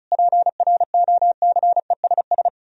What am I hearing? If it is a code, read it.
Morse code, heard as a keyed tone, about 42 wpm, 690 Hz. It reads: PROCESS
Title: Speed-Racing - Sets of 3 Words 35wpm